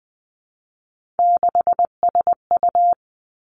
Morse code, heard as a keyed tone, about 20 wpm, 700 Hz. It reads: 6SU
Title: Three Challenging Letter-Number Combinations 20wpm